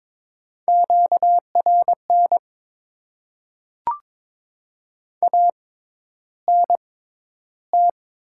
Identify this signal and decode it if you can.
Morse code, heard as a keyed tone, about 23 wpm, spaced as for 11 wpm, 700 Hz. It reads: QRN ANT
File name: Top 90 QSO Elements - 6x Character Spacing - 22wpm